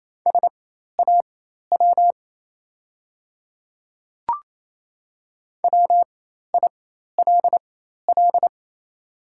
Morse code, heard as a keyed tone, about 28 wpm, spaced as for 13 wpm, 700 Hz. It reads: SAW WILL
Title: 4x Character Spacing - Top 500 Words - 28wpm